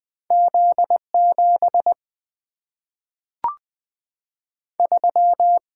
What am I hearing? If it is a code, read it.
Morse code, heard as a keyed tone, about 20 wpm, 700 Hz. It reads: Z7 3
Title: Single-Letter Single-Number 20wpm